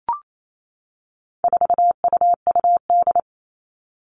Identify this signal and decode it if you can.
Morse code, heard as a keyed tone, about 28 wpm, 700 Hz. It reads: 4UUD
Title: All Cut Numbers - 4-Digit 28wpm